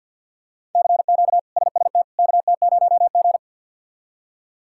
Morse code, heard as a keyed tone, about 53 wpm, 700 Hz. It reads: CQ SST KT0G
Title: K1USN SST QSO Practice - 50wpm